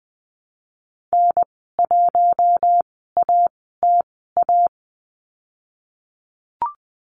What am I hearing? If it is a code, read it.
Morse code, heard as a keyed tone, about 20 wpm, spaced as for 14 wpm, 700 Hz. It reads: N1ATA
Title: 055 Lesson 10g - Letter r - Call Signs 20wpm